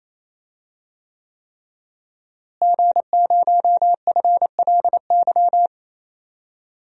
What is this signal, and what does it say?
Morse code, heard as a keyed tone, about 28 wpm, 700 Hz. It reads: G0FLY
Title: DXCC-100 Call Signs - 5 Characters - Repeated 3x - 28wpm